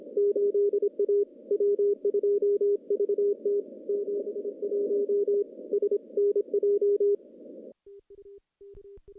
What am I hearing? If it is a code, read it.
Morse code, heard as a keyed tone, about 24 wpm, 410 Hz. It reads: 8A W2VT 71 SNJ
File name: contests / CW-SS-k5tr- -0100